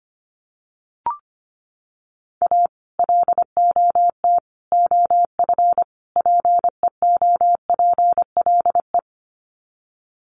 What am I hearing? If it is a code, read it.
Morse code, heard as a keyed tone, about 25 wpm, 700 Hz. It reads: A LOT OF PEOPLE
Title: Sets of 4 Words 25wpm